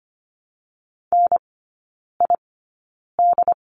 Morse code, heard as a keyed tone, about 25 wpm, spaced as for 9 wpm, 700 Hz. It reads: NID
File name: Lesson 15e - Number 5 - Three Characters 25wpm